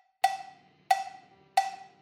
There is no code.